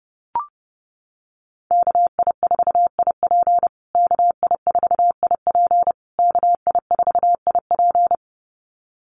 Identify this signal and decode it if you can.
Morse code, heard as a keyed tone, about 30 wpm, 700 Hz. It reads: KI4IP KI4IP KI4IP
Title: USA Call Signs Repeated 3x 30wpm